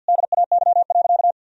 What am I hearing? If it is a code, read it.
Morse code, heard as a keyed tone, about 46 wpm, 700 Hz. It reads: DAY.
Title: Sentences from Top 300 Words and Abbreviations 50wpm